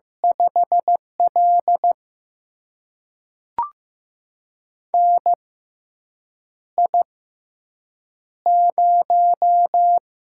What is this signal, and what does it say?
Morse code, heard as a keyed tone, about 15 wpm, spaced as for 7 wpm, 700 Hz. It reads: 5L NI0
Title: USA Call Signs - 4 Characters - 6x Character Spacing - 15wpm